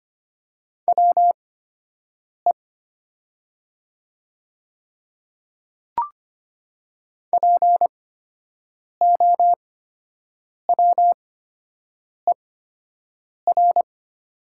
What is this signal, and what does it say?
Morse code, heard as a keyed tone, about 25 wpm, spaced as for 7 wpm, 700 Hz. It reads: WE POWER
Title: 8x Character Spacing - Top 500 Words - 25wpm